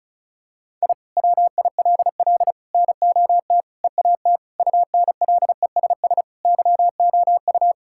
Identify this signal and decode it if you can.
Morse code, heard as a keyed tone, about 36 wpm, 700 Hz. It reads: I WILL NOT EAT UNLESS YOU
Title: Sentences from Top 1000 Words - Review Part 6 - 35wpm